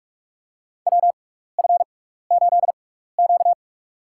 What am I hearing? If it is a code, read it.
Morse code, heard as a keyed tone, about 46 wpm, spaced as for 16 wpm, 700 Hz. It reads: WF8Q
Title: USA Call Signs - 4 Characters - 6x Character Spacing - 45wpm